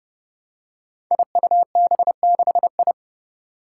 Morse code, heard as a keyed tone, about 30 wpm, 700 Hz. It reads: IUB6I
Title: Five Challenging Letter-Number Combinations 30wpm